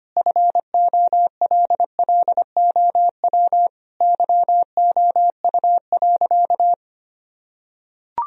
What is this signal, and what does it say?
Morse code, heard as a keyed tone, about 25 wpm, 700 Hz. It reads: FOLLOW YOU.
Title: Sentences from Top 1000 Words - Review Part 1 - 25wpm